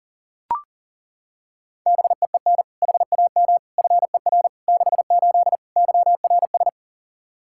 Morse code, heard as a keyed tone, about 40 wpm, 700 Hz. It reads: BEEN HAM FER 68 YRS
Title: Practice QSO - Exchange 3 - More Info - 40wpm